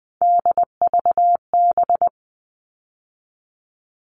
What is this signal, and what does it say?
Morse code, heard as a keyed tone, about 20 wpm, 700 Hz. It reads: DVB
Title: All Cut Numbers - 3-Digit 20wpm